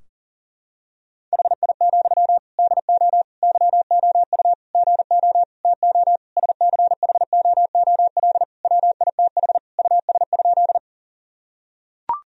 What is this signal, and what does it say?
Morse code, heard as a keyed tone, about 40 wpm, 700 Hz. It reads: HI, DO YOU GO TO SCHOOL WITH US?
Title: Sentences from Top 1000 Words - Review Part 3 - 40wpm